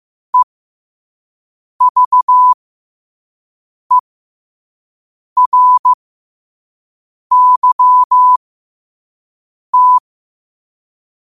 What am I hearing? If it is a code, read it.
Morse code, heard as a keyed tone, about 15 wpm, spaced as for 5 wpm, 1000 Hz. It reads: EVERYT
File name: qotdfiles / 5wpm